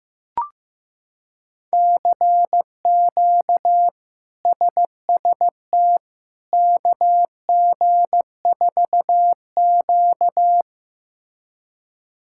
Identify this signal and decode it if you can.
Morse code, heard as a keyed tone, about 15 wpm, 700 Hz. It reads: CQ SST KG4Q
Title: K1USN SST QSO Practice - 15wpm